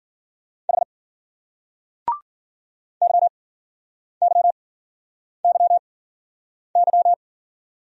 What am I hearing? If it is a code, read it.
Morse code, heard as a keyed tone, about 49 wpm, spaced as for 9 wpm, 700 Hz. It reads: H YYYY
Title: Speed-Racing Single Letters 40wpm